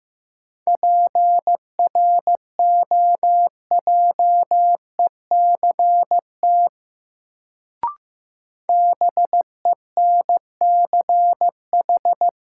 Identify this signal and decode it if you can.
Morse code, heard as a keyed tone, about 15 wpm, 700 Hz. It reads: PROJECT BENCH
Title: Sentences from Top 1400 Words - New Word Drill Alt - 15wpm